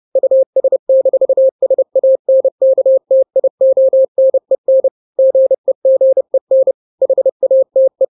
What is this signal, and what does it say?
Morse code, heard as a keyed tone, about 30 wpm, 530 Hz. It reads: US-SANKTIONEN GEGEN HATE
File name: 29wpm